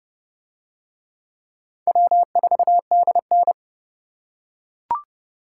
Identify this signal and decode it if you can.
Morse code, heard as a keyed tone, about 26 wpm, 700 Hz. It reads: W4DN